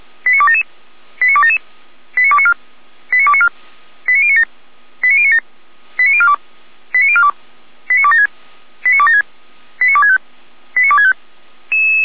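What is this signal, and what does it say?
Five-tone selective call (ZVEI), heard as quick runs of five tones, tones of about 70 ms. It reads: 89277 89277 89294 89294 89097 89097 89042 89042 89276 89276 89265 89265
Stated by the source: ZVEI five-tone call 89310 received by SDR, archived on server of Feuerwehr Orlamünde (audio.f85360.feuerwehr-orlamuende.org).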